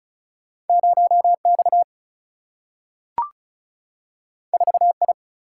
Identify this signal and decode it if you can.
Morse code, heard as a keyed tone, about 36 wpm, 700 Hz. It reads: 0 X 4 I
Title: Single-Letter Single-Number 35wpm